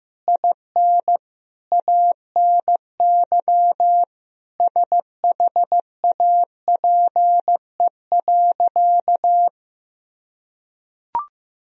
Morse code, heard as a keyed tone, about 15 wpm, 700 Hz. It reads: IN ANY SHAPE.
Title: Sentences from Top 1000 Words - Review Part 3 - 15wpm